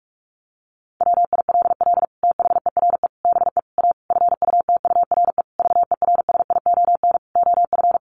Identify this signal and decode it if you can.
Morse code, heard as a keyed tone, about 44 wpm, 700 Hz. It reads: WILL THERE BE A FUTURE VERSION OF
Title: Sentences from Top 1400 Words 45wpm